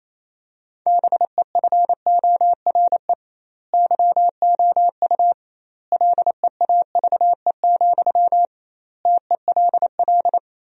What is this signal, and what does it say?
Morse code, heard as a keyed tone, about 28 wpm, 700 Hz. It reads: BEFORE YOU LEAVE, TELL
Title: 2x Word-Spacing - Sentences from Top 400 Words - 28wpm